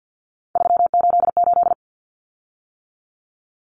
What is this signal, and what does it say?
Morse code, heard as a keyed tone, about 50 wpm, 700 Hz. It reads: 388
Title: Common Cut Numbers - 3-Digit 50wpm